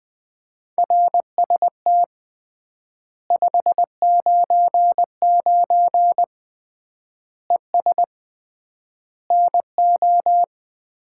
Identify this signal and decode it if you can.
Morse code, heard as a keyed tone, about 20 wpm, 700 Hz. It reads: RST 599 ES NO
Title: Common QSO Phrases - 3x Word-Spacing - 20wpm